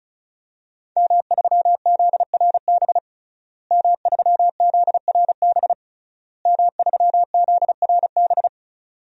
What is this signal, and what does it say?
Morse code, heard as a keyed tone, about 36 wpm, 700 Hz. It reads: M3ZRB M3ZRB M3ZRB
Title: DXCC-100 Call Signs - 5 Characters - Repeated 3x - 35wpm